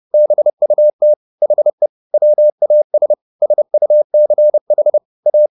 Morse code, heard as a keyed tone, about 30 wpm, 600 Hz. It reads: BUT HE WAS SUCH A